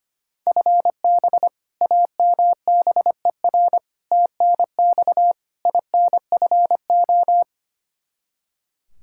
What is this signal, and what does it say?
Morse code, heard as a keyed tone, about 25 wpm, 700 Hz. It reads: FB AMBER TNX INFO
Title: Practice QSO - Exchange 3 - More Info - 25wpm